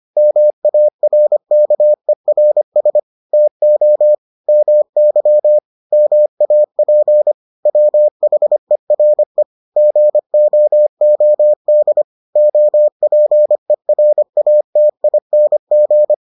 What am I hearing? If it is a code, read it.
Morse code, heard as a keyed tone, about 25 wpm, 600 Hz. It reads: MARKERS TO MY MAP WHERE GOOD OPERATING